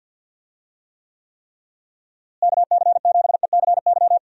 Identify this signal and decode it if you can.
Morse code, heard as a keyed tone, about 51 wpm, 700 Hz. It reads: KC7ECY
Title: USA Call Signs - 6 Characters - Repeated 3x - 50wpm